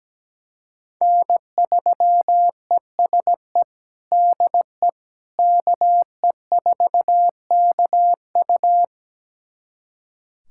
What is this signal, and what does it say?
Morse code, heard as a keyed tone, about 17 wpm, 700 Hz. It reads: N3ESE DE KE4KU
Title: USA Callsign DE Callsign - 5 Characters - 17wpm